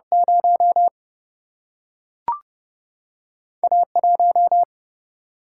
Morse code, heard as a keyed tone, about 30 wpm, 700 Hz. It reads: 0 A 1
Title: Single-Letter Single-Number 30wpm